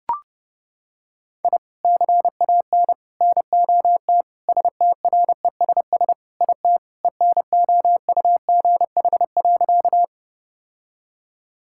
Morse code, heard as a keyed tone, about 30 wpm, 700 Hz. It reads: I CAN NOT STRESS IT ENOUGH.
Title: Sentences from Top 1000 Words - Review Part 6 - 30wpm